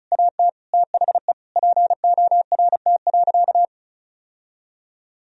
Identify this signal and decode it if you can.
Morse code, heard as a keyed tone, about 34 wpm, 700 Hz. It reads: AT THE PORT.
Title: Sentences from Top 1000 Words - Review Part 1 - 35wpm